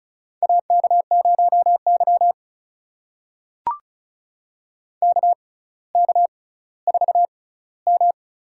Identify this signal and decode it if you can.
Morse code, heard as a keyed tone, about 34 wpm, spaced as for 18 wpm, 700 Hz. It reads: AK0Y KK4M
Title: USA Call Signs - 6x Character Spacing - 35wpm